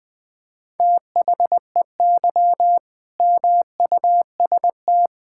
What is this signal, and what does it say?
Morse code, heard as a keyed tone, about 20 wpm, 700 Hz. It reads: THEY MUST
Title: Sentences from Top 1000 Words - Review Part 6 - 20wpm